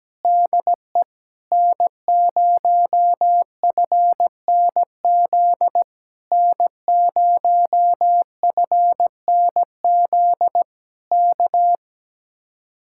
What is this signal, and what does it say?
Morse code, heard as a keyed tone, about 17 wpm, 700 Hz. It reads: DE N0FNZ N0FNZ K
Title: Speed-Racing Common QSO Phrases - 17wpm